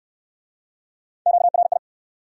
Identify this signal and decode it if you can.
Morse code, heard as a keyed tone, about 62 wpm, 700 Hz. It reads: QRI